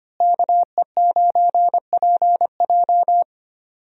Morse code, heard as a keyed tone, about 25 wpm, 700 Hz. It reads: KE9PJ
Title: USA Call Signs - 5 Characters - Repeated 2x - 25wpm